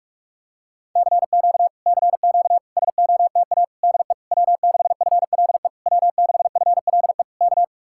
Morse code, heard as a keyed tone, about 49 wpm, 700 Hz. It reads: CQ CQ SOTA DE W6FLE W6FLE K